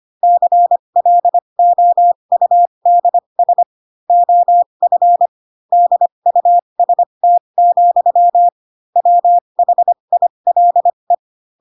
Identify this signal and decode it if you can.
Morse code, heard as a keyed tone, about 25 wpm, spaced as for 22 wpm, 700 Hz. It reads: CLOUDS OF DUST, WHILE
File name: Pinocchio / Pinocchio ch6 18 25 ik1hge